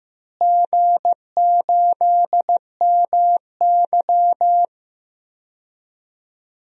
Morse code, heard as a keyed tone, about 15 wpm, 700 Hz. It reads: G8MY
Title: DXCC-100 Call Signs - 4 Characters - Repeated 3x - 15wpm